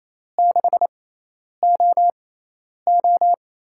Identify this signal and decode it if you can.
Morse code, heard as a keyed tone, about 28 wpm, spaced as for 10 wpm, 700 Hz. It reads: 6OO
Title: USA Call Signs - 4 Characters - 6x Character Spacing - 28wpm